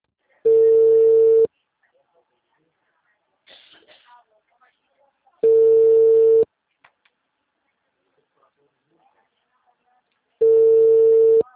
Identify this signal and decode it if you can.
Ringback tone (China, 450 Hz), with speech.
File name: out- -2009- - - .263